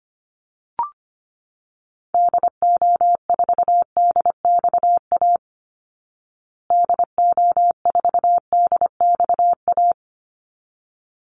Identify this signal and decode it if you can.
Morse code, heard as a keyed tone, about 25 wpm, 700 Hz. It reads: DO4DXA DO4DXA
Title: Call Signs Repeated 3x - 25wpm